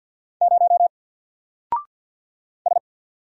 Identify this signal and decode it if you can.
Morse code, heard as a keyed tone, about 48 wpm, 700 Hz. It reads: 0 S
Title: Single Character (Letter:Number:Punctuation:Prosign) 50wpm